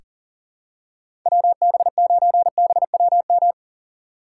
Morse code, heard as a keyed tone, about 40 wpm, 700 Hz. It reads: WB9BWM
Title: USA Call Signs - 6 Characters - 2x Character Spacing - 40wpm